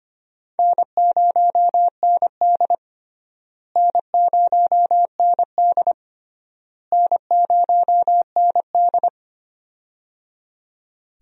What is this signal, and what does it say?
Morse code, heard as a keyed tone, about 25 wpm, 700 Hz. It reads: N0ND N0ND N0ND
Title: USA Call Signs - 4 Characters - Repeated 3x - 25wpm